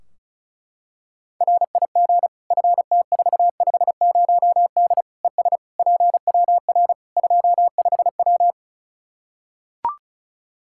Morse code, heard as a keyed tone, about 36 wpm, 700 Hz. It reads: RIG FT450D ES PWR 25W
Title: Practice QSO - Exchange 2 - Info - 35wpm